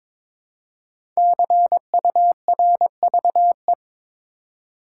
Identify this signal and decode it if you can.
Morse code, heard as a keyed tone, about 22 wpm, 700 Hz. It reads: CURVE
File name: 5 Letter Words - First 3 Letters Given - 22wpm